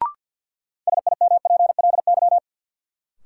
Morse code, heard as a keyed tone, about 47 wpm, 700 Hz. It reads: SIMPLY
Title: Sentences from Top 600 Words - New Word Drill - 50wpm